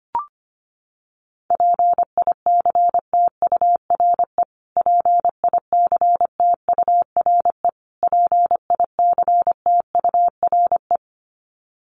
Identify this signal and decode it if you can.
Morse code, heard as a keyed tone, about 25 wpm, 700 Hz. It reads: PICTURE PICTURE PICTURE
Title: Top 300 Words Repeated 25wpm